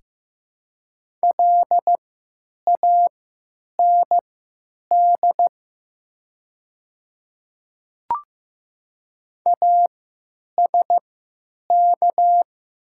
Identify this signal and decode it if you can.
Morse code, heard as a keyed tone, about 15 wpm, spaced as for 9 wpm, 700 Hz. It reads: LAND ASK
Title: 3x Character Spacing - Top 200 Words - 15wpm